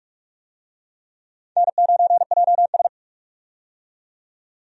Morse code, heard as a keyed tone, about 45 wpm, 700 Hz. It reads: N9JS
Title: USA Call Signs - 4 Characters - 45wpm